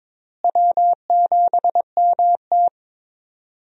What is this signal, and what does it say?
Morse code, heard as a keyed tone, about 22 wpm, 700 Hz. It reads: W7MT
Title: USA Call Signs - 4 Characters - Repeated 2x - 22wpm